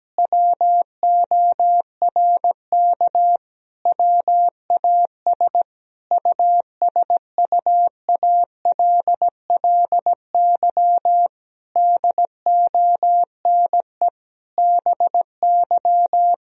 Morse code, heard as a keyed tone, about 17 wpm, 700 Hz. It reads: WORK WAS USUALLY DONE BY
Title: Sentences from Top 1000 Words - Review Part 3 - 17wpm